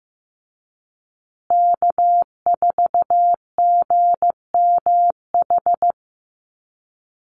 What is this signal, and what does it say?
Morse code, heard as a keyed tone, about 14 wpm, 700 Hz. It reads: K4GMH
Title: Call Signs Repeated 2x - 15wpm